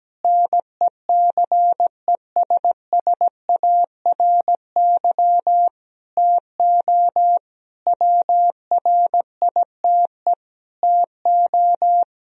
Morse code, heard as a keyed tone, about 17 wpm, 700 Hz. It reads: NECESSARY TO WRITE TO
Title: Sentences from Top 1000 Words - Review Part 1 - 17wpm